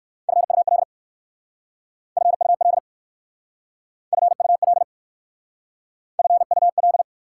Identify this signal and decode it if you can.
Morse code, heard as a keyed tone, about 54 wpm, 700 Hz. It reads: FUL FUL FUL FUL
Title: Speed-Racing Prefixes and Suffixes 45wpm